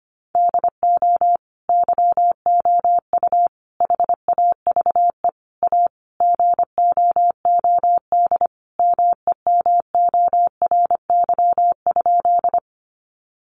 Morse code, heard as a keyed tone, about 25 wpm, 700 Hz. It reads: DO YOU HAVE A GOOD MEMORY?
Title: Sentences from Top 1300 Words 25wpm